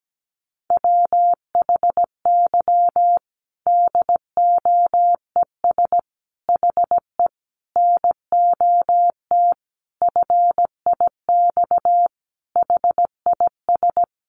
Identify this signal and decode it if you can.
Morse code, heard as a keyed tone, about 17 wpm, 700 Hz. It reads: WHY DOES HE NOT FIX HIS
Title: Sentences from Top 1400 Words 17wpm